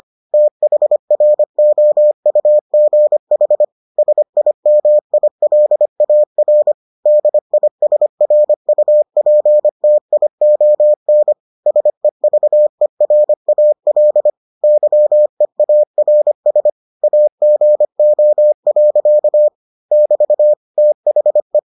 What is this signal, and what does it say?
Morse code, heard as a keyed tone, about 25 wpm, 600 Hz. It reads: THROUGH SIMILAR DISRUPTION SEVERAL YEARS AGO. <BT> THE